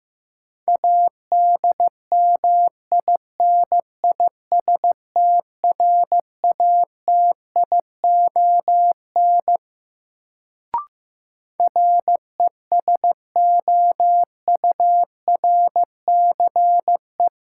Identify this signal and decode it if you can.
Morse code, heard as a keyed tone, about 15 wpm, 700 Hz. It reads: ADMINISTRATION RESOURCE
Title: Sentences from Top 1300 Words - New Word Drill - 15wpm